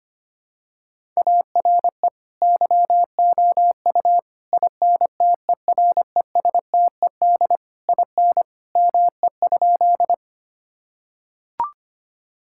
Morse code, heard as a keyed tone, about 25 wpm, 700 Hz. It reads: ARE YOU INTERESTED IN ME?
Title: Sentences from Top 1000 Words - Review Part 6 - 25wpm